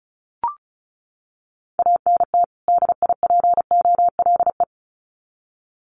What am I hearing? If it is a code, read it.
Morse code, heard as a keyed tone, about 34 wpm, 700 Hz. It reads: ANT DIPOLE
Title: Common QSO Phrases 35wpm